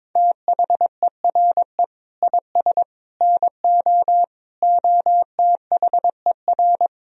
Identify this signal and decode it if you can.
Morse code, heard as a keyed tone, about 21 wpm, 700 Hz. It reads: THERE IS NO OTHER
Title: Sentences from Top 1000 Words - Review Part 4 - 22wpm